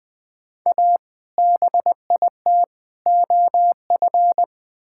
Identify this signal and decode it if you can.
Morse code, heard as a keyed tone, about 21 wpm, 700 Hz. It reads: A BIT OF
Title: Build A Phrase - 4-Word Phrases - 20wpm